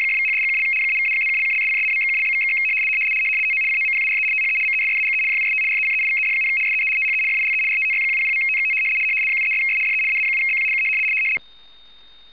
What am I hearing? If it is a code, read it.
RTTY, about 100 baud. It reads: !=#$&() -=-+()()':',./()?=// TO COMPARE THE MESSAGE JUST RECEIVED, SEE THE FILE RTTY100.TXT 73'S DE AA7BQ K
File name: w ry425 / rtty100.wav / rtty100